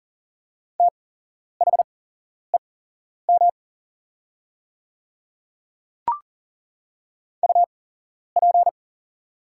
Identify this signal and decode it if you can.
Morse code, heard as a keyed tone, about 40 wpm, spaced as for 11 wpm, 690 Hz. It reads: THEM UP
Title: 8x Character Spacing - Top 200 Words - 40wpm